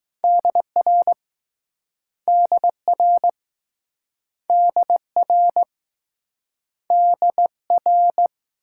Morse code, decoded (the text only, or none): DR DR DR DR